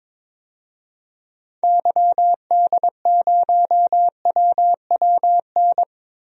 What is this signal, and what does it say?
Morse code, heard as a keyed tone, about 22 wpm, 700 Hz. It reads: YD0WWN
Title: DXCC-100 Call Signs - 6 Characters - Repeated 3x - 22wpm